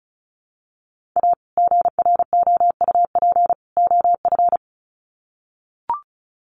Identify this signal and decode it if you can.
Morse code, heard as a keyed tone, about 34 wpm, 700 Hz. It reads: A GROUP OF
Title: Sets of 3 Words 35wpm